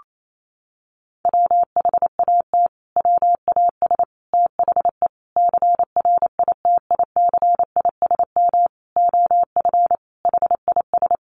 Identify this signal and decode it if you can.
Morse code, heard as a keyed tone, about 28 wpm, 700 Hz. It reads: WHAT WAS THE CRITICISM OF HIS
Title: Sentences from Top 1300 Words 28wpm